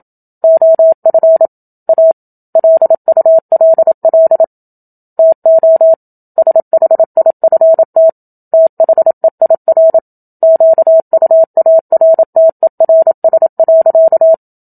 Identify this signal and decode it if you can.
Morse code, heard as a keyed tone, about 27 wpm, 650 Hz. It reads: OF A LULL TO SHIFT THEIR QUARTERS.
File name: Book Two / 01-Chapter 1 - 27 WPM